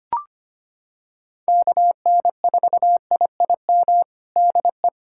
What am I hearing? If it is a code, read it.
Morse code, heard as a keyed tone, about 25 wpm, 700 Hz. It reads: KN4IIM DE
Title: USA Callsign DE Callsign 25wpm